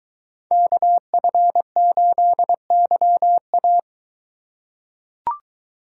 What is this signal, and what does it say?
Morse code, heard as a keyed tone, about 21 wpm, 700 Hz. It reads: KF8YA